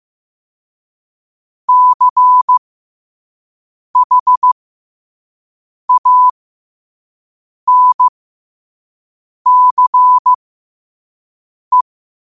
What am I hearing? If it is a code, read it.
Morse code, heard as a keyed tone, about 15 wpm, spaced as for 5 wpm, 1000 Hz. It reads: CHANCE
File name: qotdfiles / 5wpm